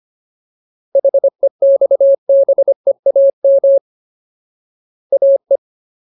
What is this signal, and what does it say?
Morse code, heard as a keyed tone, about 25 wpm, 550 Hz. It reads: HEXBEAM AE
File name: Ultimate Word List / 25WPM Ultimate